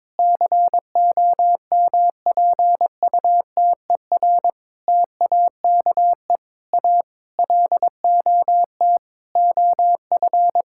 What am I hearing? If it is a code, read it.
Morse code, heard as a keyed tone, about 22 wpm, 700 Hz. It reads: COMPUTER TAKE A LOT OF
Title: Sentences from Top 1000 Words - Review Part 6 - 22wpm